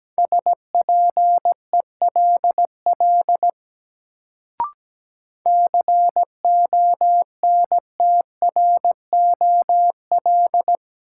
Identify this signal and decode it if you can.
Morse code, heard as a keyed tone, about 17 wpm, 700 Hz. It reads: SPELL CONTROL
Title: Sentences from Top 1200 Words - New Word Drill Alt - 17wpm